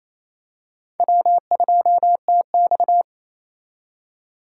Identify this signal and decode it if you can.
Morse code, heard as a keyed tone, about 27 wpm, 700 Hz. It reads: W2TX
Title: K1USN SST QSO Practice - 28wpm